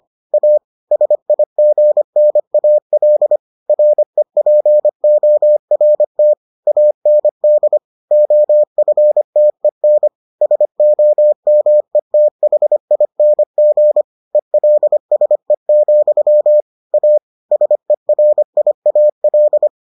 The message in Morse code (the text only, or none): A SIGNAL REPORT AND OFTEN SOMETHING ELSE, A SERIAL